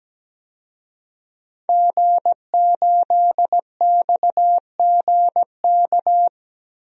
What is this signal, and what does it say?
Morse code, heard as a keyed tone, about 17 wpm, 700 Hz. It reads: G8XGK
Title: DXCC-100 Call Signs - 5 Characters - 17wpm